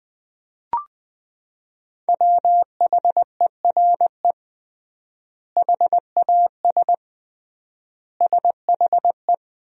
Morse code, heard as a keyed tone, about 20 wpm, 700 Hz. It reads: WHERE HAS SHE
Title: 3x Word-Spacing - Sentences from Top 300 Words - 20wpm